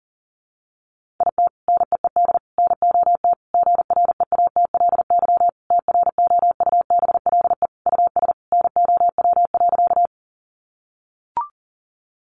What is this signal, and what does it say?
Morse code, heard as a keyed tone, about 40 wpm, 700 Hz. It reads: IT NEED NOT GREATLY TROUBLE US NOW.
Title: Sentences from Top 1400 Words 40wpm